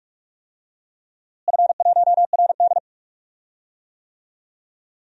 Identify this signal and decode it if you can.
Morse code, heard as a keyed tone, about 42 wpm, 700 Hz. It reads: F1RD
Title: DXCC-100 Call Signs - 4 Characters - Repeated 3x - 45wpm